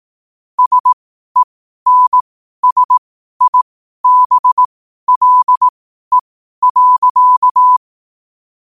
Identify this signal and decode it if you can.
Morse code, heard as a keyed tone, about 18 wpm, spaced as for 12 wpm, 1000 Hz. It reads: SENSIBLE.
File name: qotdfiles / 12wpm